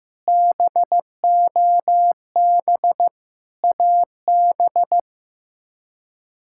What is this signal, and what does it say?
Morse code, heard as a keyed tone, about 15 wpm, 700 Hz. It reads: BOB AB
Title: K1USN SST QSO Practice - 15wpm